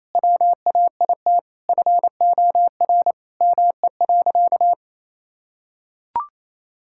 Morse code, heard as a keyed tone, about 28 wpm, 700 Hz. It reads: WAIT FOR ME.
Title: Sentences from Top 1000 Words - Review Part 6 - 28wpm